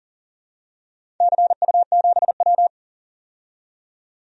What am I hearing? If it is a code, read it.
Morse code, heard as a keyed tone, about 40 wpm, 700 Hz. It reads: CU7W
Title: DXCC-100 Call Signs - 4 Characters - 40wpm